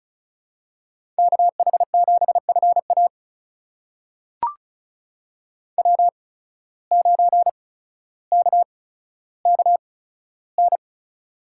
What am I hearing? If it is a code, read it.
Morse code, heard as a keyed tone, about 36 wpm, spaced as for 15 wpm, 700 Hz. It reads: KH7FA W9KKN
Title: USA Call Signs - 5 Characters - 8x Character Spacing - 35wpm